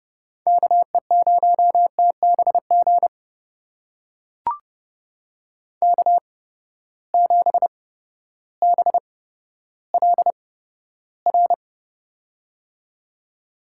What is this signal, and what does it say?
Morse code, heard as a keyed tone, about 31 wpm, spaced as for 14 wpm, 700 Hz. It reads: KE0TBG K7BLR
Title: USA Call Signs - 8x Character Spacing - 30wpm